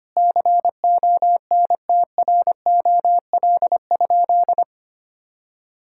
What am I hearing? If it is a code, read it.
Morse code, heard as a keyed tone, about 25 wpm, 700 Hz. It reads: CONTROL?
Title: Sentences from Top 1000 Words - Review Part 5 - 25wpm